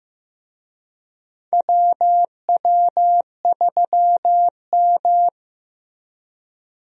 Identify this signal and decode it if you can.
Morse code, heard as a keyed tone, about 15 wpm, 700 Hz. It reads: WW3M
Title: USA Call Signs - 4 Characters - Repeated 2x - 15wpm